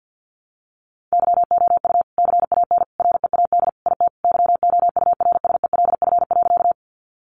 Morse code, heard as a keyed tone, about 50 wpm, 700 Hz. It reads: YOU CAN READ IT YOURSELF.
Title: Sentences from Top 600 Words 50wpm